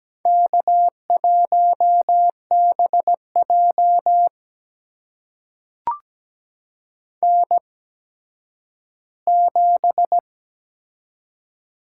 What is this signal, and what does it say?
Morse code, heard as a keyed tone, about 17 wpm, spaced as for 10 wpm, 700 Hz. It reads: K1BJ N7
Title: USA Call Signs - 4 Characters - 8x Character Spacing - 17wpm